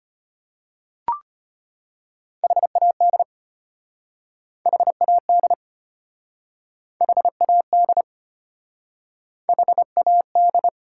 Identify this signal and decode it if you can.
Morse code, heard as a keyed tone, about 33 wpm, 700 Hz. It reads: HAD HAD HAD HAD